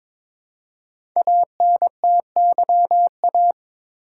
Morse code, heard as a keyed tone, about 22 wpm, 700 Hz. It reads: ANTYA
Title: Code Groups - 5 Letter:Numbers - 22wpm